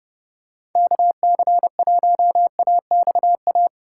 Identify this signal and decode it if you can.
Morse code, heard as a keyed tone, about 30 wpm, 700 Hz. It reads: KC1AXA